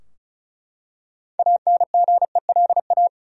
Morse code, heard as a keyed tone, about 36 wpm, 700 Hz. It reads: ANGELA
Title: USA Common Names - 35wpm